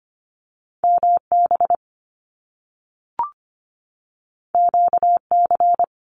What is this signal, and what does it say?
Morse code, heard as a keyed tone, about 25 wpm, 700 Hz. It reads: MB QC
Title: Canadian Provinces and Territories 25wpm